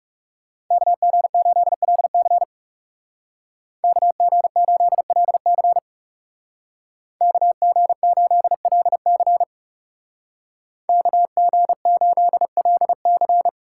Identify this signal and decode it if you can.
Morse code, heard as a keyed tone, about 39 wpm, 700 Hz. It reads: KG8LC KG8LC KG8LC KG8LC